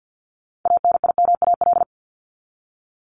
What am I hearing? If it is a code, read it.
Morse code, heard as a keyed tone, about 50 wpm, 700 Hz. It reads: ANIMAL
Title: ES:S Suffix - Top 100 Words with Base Words 50wpm